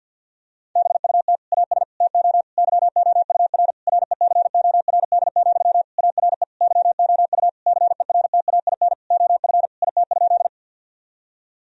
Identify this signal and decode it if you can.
Morse code, heard as a keyed tone, about 52 wpm, 700 Hz. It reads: BUT AS TO YOUR RECORD, ARE YOU CERTAIN OF IT?
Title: Sentences from Top 1000 Words - Review Part 3 - 50wpm